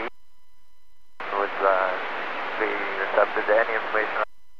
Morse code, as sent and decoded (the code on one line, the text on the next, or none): none